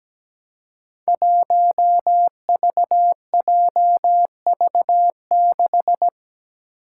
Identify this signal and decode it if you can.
Morse code, heard as a keyed tone, about 17 wpm, 700 Hz. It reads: 1VJV6
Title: Five Challenging Letter-Number Combinations 17wpm